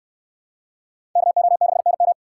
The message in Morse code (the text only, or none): KO6NM